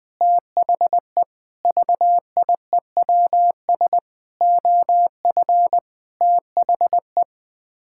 Morse code, heard as a keyed tone, about 21 wpm, 700 Hz. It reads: THE VIEWS OF THE
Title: Sets of 8 Words - 20wpm